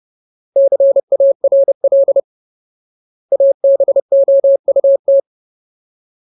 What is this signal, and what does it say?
Morse code, heard as a keyed tone, about 30 wpm, 550 Hz. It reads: CARL ABOUT
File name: Ultimate Word List / 30WPM Ultimate